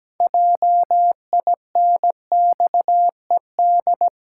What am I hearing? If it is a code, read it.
Morse code, heard as a keyed tone, about 17 wpm, 700 Hz. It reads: JINXED
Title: Pangram Challenge v2 - 17wpm